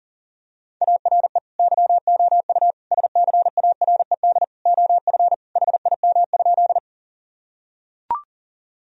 Morse code, heard as a keyed tone, about 40 wpm, 700 Hz. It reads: ARE YOU SCARED OF HIM?
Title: Sentences from Top 1000 Words - Review Part 4 - 40wpm